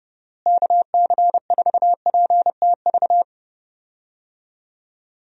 Morse code, heard as a keyed tone, about 30 wpm, 700 Hz. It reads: KC4PTV
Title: USA Call Signs - 6 Characters - Repeated 2x - 30wpm